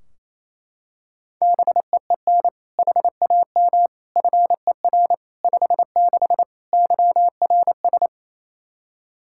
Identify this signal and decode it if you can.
Morse code, heard as a keyed tone, about 28 wpm, 700 Hz. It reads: BEEN HAM FER 56 YRS
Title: Practice QSO - Exchange 3 - More Info - 28wpm